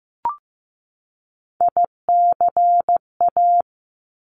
Morse code, heard as a keyed tone, about 15 wpm, 700 Hz. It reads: ICA
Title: Top Three Letter Combinations in Words 15wpm